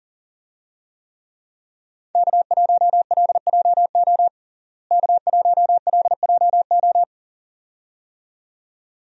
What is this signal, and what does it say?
Morse code, heard as a keyed tone, about 40 wpm, 700 Hz. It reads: K1LJO K1LJO
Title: USA Call Signs - 5 Characters - Repeated 2x - 40wpm